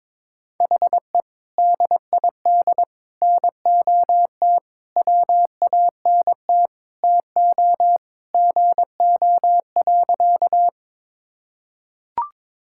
Morse code, heard as a keyed tone, about 22 wpm, 700 Hz. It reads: HE DID NOT WANT TO GO.
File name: practice / 22wpm / Sentences with 4-Word Phrases - 22wpm